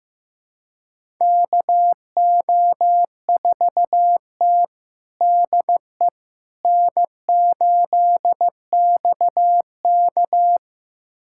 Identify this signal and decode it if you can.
Morse code, heard as a keyed tone, about 16 wpm, 700 Hz. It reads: KO4T DE N8XK
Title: USA Callsign DE Callsign - 4 Characters - 15wpm